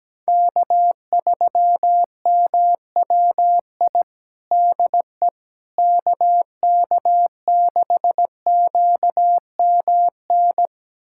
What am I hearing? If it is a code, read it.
Morse code, heard as a keyed tone, about 17 wpm, 700 Hz. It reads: K3MWI DE KK6QMN
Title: USA Callsign DE Callsign 17wpm